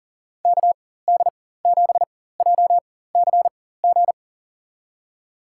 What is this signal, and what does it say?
Morse code, heard as a keyed tone, about 40 wpm, spaced as for 19 wpm, 700 Hz. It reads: KD7JCG
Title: USA Call Signs - 6 Characters - 4x Character Spacing - 40wpm